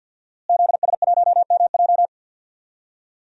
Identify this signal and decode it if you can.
Morse code, heard as a keyed tone, about 52 wpm, 700 Hz. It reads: ZS1MJ
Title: DXCC-100 Call Signs - 5 Characters - Repeated 2x - 50wpm